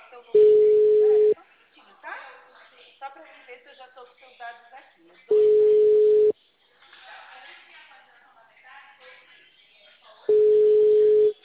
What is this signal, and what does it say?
Ringback tone (Europe, 425 Hz), with speech.
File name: out- -2002- - - .3687